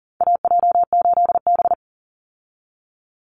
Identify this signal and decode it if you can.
Morse code, heard as a keyed tone, about 40 wpm, 700 Hz. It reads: AJ8B
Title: Call Signs - 40wpm